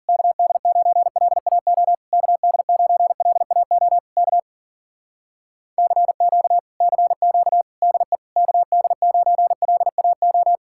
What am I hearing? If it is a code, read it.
Morse code, heard as a keyed tone, about 52 wpm, 700 Hz. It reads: KD9LAO KD9LAO K CQ CQ DE KD9LAO